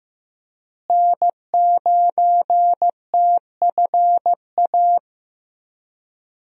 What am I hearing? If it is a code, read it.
Morse code, heard as a keyed tone, about 16 wpm, 700 Hz. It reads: N9TFA
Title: USA Call Signs - 5 Characters - Repeated 3x - 15wpm